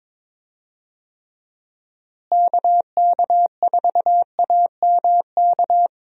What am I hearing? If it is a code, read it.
Morse code, heard as a keyed tone, about 22 wpm, 700 Hz. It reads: KK4AMK
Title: USA Call Signs - 6 Characters - Repeated 3x - 22wpm